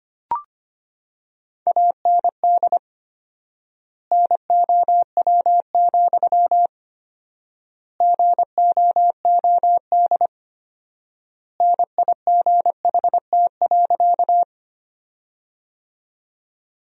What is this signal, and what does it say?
Morse code, heard as a keyed tone, about 25 wpm, 700 Hz. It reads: AND NOW, GOOD NIGHT.
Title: 4x Word-Spacing - Sentences from Top 300 Words - 25wpm